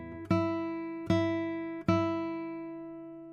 There is no code.